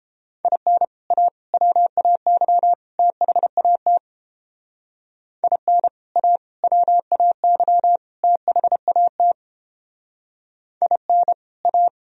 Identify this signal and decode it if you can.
Morse code, heard as a keyed tone, about 29 wpm, 700 Hz. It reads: IN A WAY THAT IN A WAY THAT IN A